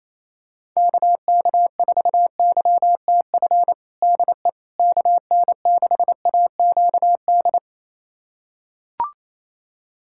Morse code, heard as a keyed tone, about 28 wpm, 700 Hz. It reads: KK4YTF DE KN6AQD
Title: USA Callsign DE Callsign - 6 Characters - 28wpm